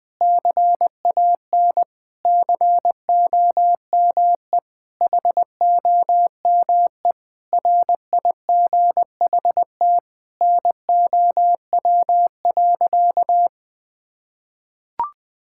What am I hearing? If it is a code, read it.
Morse code, heard as a keyed tone, about 20 wpm, 700 Hz. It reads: CAN COME HOME RIGHT NOW.
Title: Sentences with 5-Word Phrases - Spoken Beginning - 20wpm